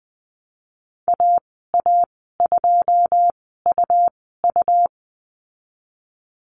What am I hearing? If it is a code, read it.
Morse code, heard as a keyed tone, about 20 wpm, spaced as for 14 wpm, 700 Hz. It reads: AA2UU
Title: 161 Lesson 26g - Letter g - Call Signs 20wpm